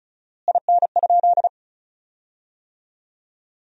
Morse code, heard as a keyed tone, about 34 wpm, 700 Hz. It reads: IN?
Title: 4x Word-Spacing - Sentences from Top 100 Words - 35wpm